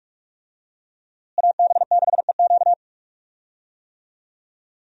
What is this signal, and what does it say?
Morse code, heard as a keyed tone, about 46 wpm, 700 Hz. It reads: AB6EQ
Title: USA Call Signs - 5 Characters - Repeated 3x - 45wpm